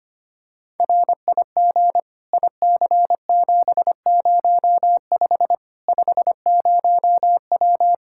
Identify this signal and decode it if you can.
Morse code, heard as a keyed tone, about 25 wpm, 700 Hz. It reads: RIG IC705 50W
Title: Practice QSO - Exchange 2 - Info - 25wpm